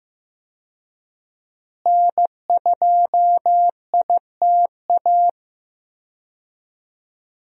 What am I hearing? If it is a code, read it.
Morse code, heard as a keyed tone, about 15 wpm, 700 Hz. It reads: N2ITA
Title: USA Call Signs - 5 Characters - Repeated 3x - 15wpm